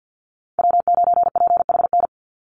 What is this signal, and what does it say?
Morse code, heard as a keyed tone, about 50 wpm, 700 Hz. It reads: W9PHN
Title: USA Call Signs 50wpm